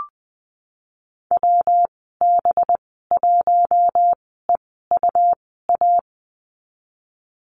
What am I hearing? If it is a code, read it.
Morse code, heard as a keyed tone, about 20 wpm, spaced as for 15 wpm, 700 Hz. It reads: WB1EUA
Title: 186 Lesson 30g - Letter b - Call Signs 20wpm